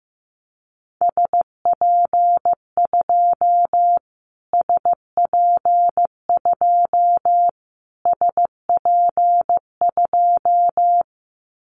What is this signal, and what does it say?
Morse code, heard as a keyed tone, about 15 wpm, 700 Hz. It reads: SP2 SP2 SP2
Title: Callsign Prefixes for Top 25 DXCC - Repeated 3x - 15wpm